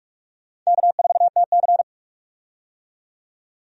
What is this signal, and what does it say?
Morse code, heard as a keyed tone, about 46 wpm, 700 Hz. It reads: K4TC
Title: USA Call Signs - 4 Characters - 45wpm